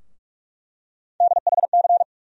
Morse code, heard as a keyed tone, about 46 wpm, 700 Hz. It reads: DHC
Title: USA Call Sign Postfixes - 3 Letters 45wpm